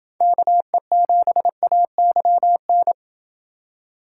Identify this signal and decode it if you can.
Morse code, heard as a keyed tone, about 29 wpm, 700 Hz. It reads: KE7AYN